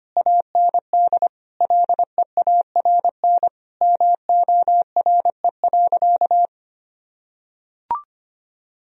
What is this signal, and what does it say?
Morse code, heard as a keyed tone, about 25 wpm, 700 Hz. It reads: AND LEARN MORE.
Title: Sentences from Top 1000 Words - Review Part 4 - 25wpm